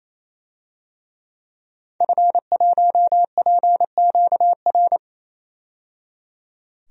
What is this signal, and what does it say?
Morse code, heard as a keyed tone, about 28 wpm, 700 Hz. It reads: F1PQR
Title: DXCC-100 Call Signs - 5 Characters - 28wpm